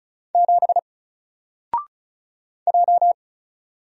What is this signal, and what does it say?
Morse code, heard as a keyed tone, about 36 wpm, 700 Hz. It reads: 7 J
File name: Single-Letter-Number-35wpm